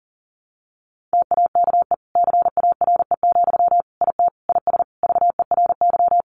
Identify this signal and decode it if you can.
Morse code, heard as a keyed tone, about 40 wpm, 700 Hz. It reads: TAKE CARE, IT IS VERY
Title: Sentences from Top 1100 Words 40wpm